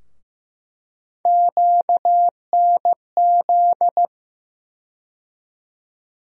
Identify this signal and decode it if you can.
Morse code, heard as a keyed tone, about 15 wpm, 700 Hz. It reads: QNZ
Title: USA Call Sign Postfixes - 3 Letters 15wpm